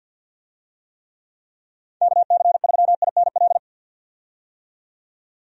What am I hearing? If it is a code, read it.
Morse code, heard as a keyed tone, about 52 wpm, 700 Hz. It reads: KC3INL
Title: USA Call Signs - 6 Characters - 50wpm